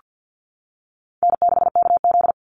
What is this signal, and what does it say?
Morse code, heard as a keyed tone, about 48 wpm, 700 Hz. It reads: N6KZ
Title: NA SOTA Chaser Callsigns 50wpm